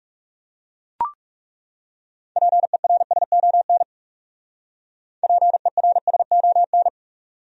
Morse code, heard as a keyed tone, about 38 wpm, 700 Hz. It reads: PERSON PERSON